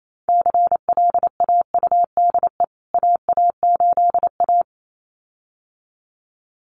Morse code, heard as a keyed tone, about 28 wpm, 700 Hz. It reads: CLAUDE AA8A
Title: CWT Exchange 28wpm